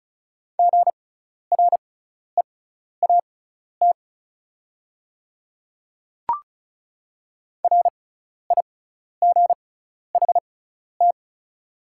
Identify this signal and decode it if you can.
Morse code, heard as a keyed tone, about 34 wpm, spaced as for 12 wpm, 700 Hz. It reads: GREAT RIGHT
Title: 6x Character Spacing - Top 200 Words - 35wpm